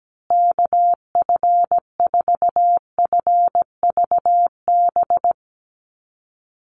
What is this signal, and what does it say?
Morse code, heard as a keyed tone, about 17 wpm, 700 Hz. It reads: KF4FVB
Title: USA Call Signs Repeated 3x 17wpm